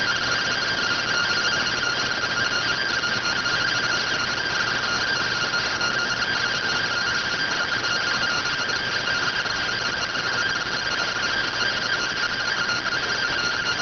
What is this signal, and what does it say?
Two-tone teleprinter signal, about 170 Hz shift.